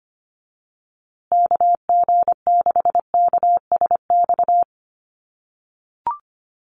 Morse code, heard as a keyed tone, about 25 wpm, 700 Hz. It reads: KG6KSX
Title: USA Call Signs Repeated 3x 25wpm